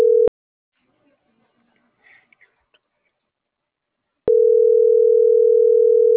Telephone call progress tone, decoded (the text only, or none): ringback tone